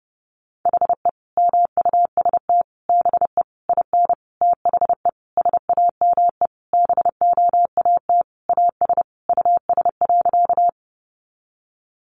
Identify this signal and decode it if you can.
Morse code, heard as a keyed tone, about 30 wpm, 700 Hz. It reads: HE MUST BE IN THE SAME BOAT AS US.
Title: Sentences from Top 500 Words 30wpm